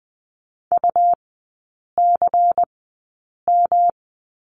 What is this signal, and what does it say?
Morse code, heard as a keyed tone, about 20 wpm, spaced as for 8 wpm, 700 Hz. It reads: UCM
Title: 159 Lesson 26e - Letter g - Three Characters 20wpm